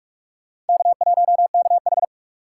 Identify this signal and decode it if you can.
Morse code, heard as a keyed tone, about 56 wpm, 700 Hz. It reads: K1KH
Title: Speed Racing - USA 1x2 Call Signs 45wpm